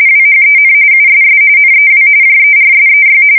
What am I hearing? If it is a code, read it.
RTTY, about 45.45 baud. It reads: WHEN USS NEW JERSEY
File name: fsk170 / USS-New-Jersey